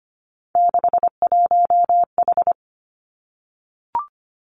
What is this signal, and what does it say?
Morse code, heard as a keyed tone, about 25 wpm, 700 Hz. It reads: 61H
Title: Three Challenging Letter-Number Combinations 25wpm